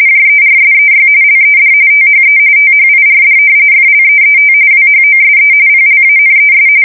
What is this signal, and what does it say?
RTTY, about 45.45 baud. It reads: JERSEY LED THE THIRD FLEET WHOSE PLANES
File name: fsk170 / battleshipnj